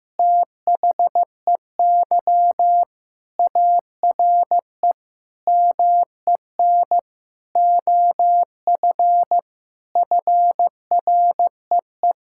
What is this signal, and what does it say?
Morse code, heard as a keyed tone, about 15 wpm, 700 Hz. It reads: THEY ARE MEN OF FREE
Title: Sentences from Top 1000 Words - Review Part 5 - 15wpm